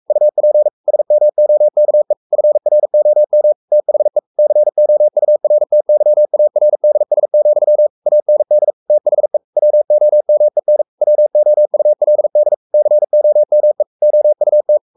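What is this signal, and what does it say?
Morse code, heard as a keyed tone, about 44 wpm, 600 Hz. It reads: UP SMOKE FROM THE COURTYARDS, AND THE WOMEN WOULD COME OUT
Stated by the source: Morse code at 43 wpm sending Ken Follett's novel Never in English, from CW med Gnister.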